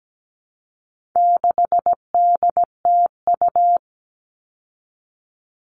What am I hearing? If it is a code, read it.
Morse code, heard as a keyed tone, about 17 wpm, 700 Hz. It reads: 6DTU
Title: All Cut Numbers - 4-Digit 17wpm